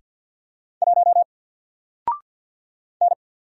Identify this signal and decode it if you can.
Morse code, heard as a keyed tone, about 50 wpm, 700 Hz. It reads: 1 N
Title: Single Character (Letter:Number:Punctuation:Prosign) 50wpm